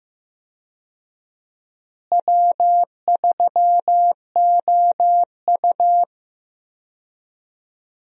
Morse code, heard as a keyed tone, about 15 wpm, 700 Hz. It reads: W3OU
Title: USA Call Signs - 4 Characters - Repeated 2x - 15wpm